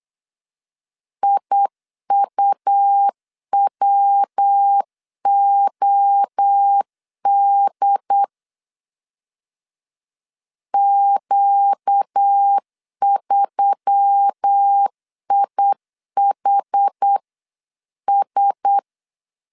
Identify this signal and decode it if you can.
Morse code, heard as a keyed tone, about 8 wpm, 800 Hz. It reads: IUWOD Q3IHS
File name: 8wpm